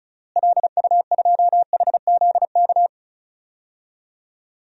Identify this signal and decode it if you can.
Morse code, heard as a keyed tone, about 36 wpm, 700 Hz. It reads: LU2HZK
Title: DXCC-100 Call Signs - 6 Characters - Repeated 3x - 35wpm